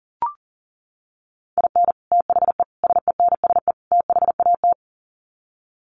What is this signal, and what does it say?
Morse code, heard as a keyed tone, about 40 wpm, 700 Hz. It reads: IN THE SENSE THAT
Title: Sets of 4 Words 40wpm